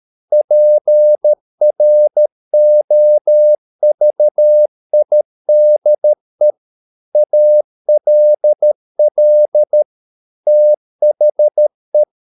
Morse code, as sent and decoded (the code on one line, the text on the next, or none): .--. .-. --- ...- .. -.. / .- .-.. .-.. / - .... .
PROVIDE ALL THE